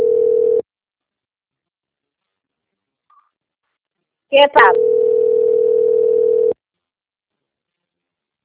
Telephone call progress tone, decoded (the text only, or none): ringback tone